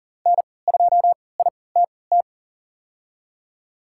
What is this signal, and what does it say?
Morse code, heard as a keyed tone, about 39 wpm, spaced as for 22 wpm, 700 Hz. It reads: N2ITT
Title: USA Call Signs - 5 Characters - 3x Character Spacing - 40wpm